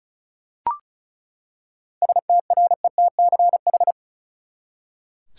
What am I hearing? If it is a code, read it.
Morse code, heard as a keyed tone, about 34 wpm, 700 Hz. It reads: STRETCH
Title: Seven Letter Words - 35wpm